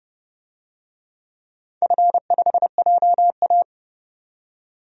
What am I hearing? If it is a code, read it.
Morse code, heard as a keyed tone, about 30 wpm, 700 Hz. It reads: F5JA
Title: DXCC-100 Call Signs - 4 Characters - 30wpm